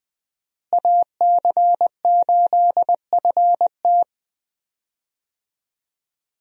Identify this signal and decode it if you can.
Morse code, heard as a keyed tone, about 20 wpm, 700 Hz. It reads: AC8FT
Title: USA Call Signs - 5 Characters - Repeated 3x - 20wpm